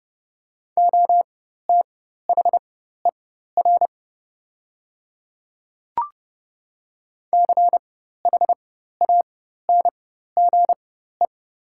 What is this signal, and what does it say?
Morse code, heard as a keyed tone, about 30 wpm, spaced as for 14 wpm, 690 Hz. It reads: OTHER CHANGE
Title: 4x Character Spacing - Top 300 Words - 30wpm